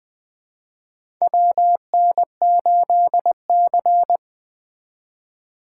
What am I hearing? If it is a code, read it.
Morse code, heard as a keyed tone, about 20 wpm, 700 Hz. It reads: WN8C
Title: USA Call Signs - 4 Characters - 4x Character Spacing - 20wpm